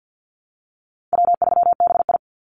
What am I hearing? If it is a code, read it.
Morse code, heard as a keyed tone, about 50 wpm, 700 Hz. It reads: W3BI
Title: Easy Call Signs Repeated 3x - 50wpm